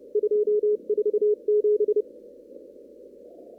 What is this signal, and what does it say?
Morse code, heard as a keyed tone, about 29 wpm, 420 Hz. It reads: 247
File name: CW / kt5j wpx cw-op-k5pi